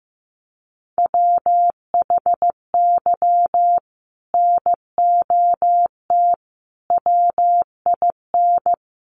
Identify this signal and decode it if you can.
Morse code, heard as a keyed tone, about 14 wpm, 700 Hz. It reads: WHY NOT WIN
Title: Sentences from Top 1500 Words 15wpm